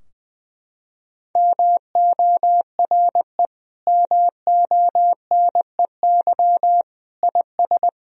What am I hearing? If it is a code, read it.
Morse code, heard as a keyed tone, about 20 wpm, 700 Hz. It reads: MORE MONEY IS
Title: Sentences from Top 1000 Words - Review Part 3 - 20wpm